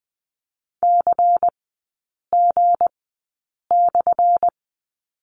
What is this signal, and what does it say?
Morse code, heard as a keyed tone, about 20 wpm, spaced as for 8 wpm, 690 Hz. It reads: CG/
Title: 191 Lesson 31e - Letter v - Three Characters 20wpm